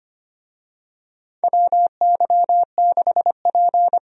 Morse code, heard as a keyed tone, about 25 wpm, 700 Hz. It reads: WY6P
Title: USA Call Signs - 4 Characters - 25wpm